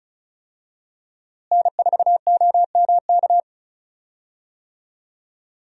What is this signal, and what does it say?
Morse code, heard as a keyed tone, about 34 wpm, 700 Hz. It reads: N4OMK
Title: USA Call Signs - 5 Characters - Repeated 3x - 35wpm